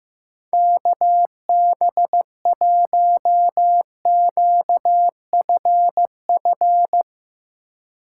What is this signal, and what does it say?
Morse code, heard as a keyed tone, about 15 wpm, 700 Hz. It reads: KB1QFF
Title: USA Call Signs - 6 Characters - 4x Character Spacing - 15wpm